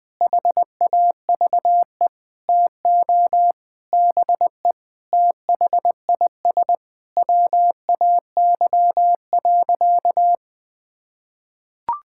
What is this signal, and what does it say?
Morse code, heard as a keyed tone, about 20 wpm, 700 Hz. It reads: HAVE TO BE THIS WAY.
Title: Sentences with 5-Word Phrases - 20wpm